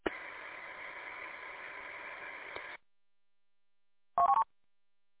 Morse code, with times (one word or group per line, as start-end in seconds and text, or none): none